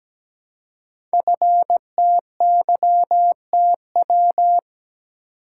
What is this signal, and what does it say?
Morse code, heard as a keyed tone, about 17 wpm, 700 Hz. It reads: FTYTW
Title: Code Groups - 5 Letters - 17wpm